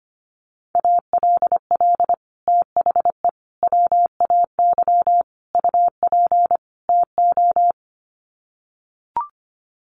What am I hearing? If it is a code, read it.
Morse code, heard as a keyed tone, about 25 wpm, 700 Hz. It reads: ALL THE WAY UP TO
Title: Sets of 5 Words Encore 25wpm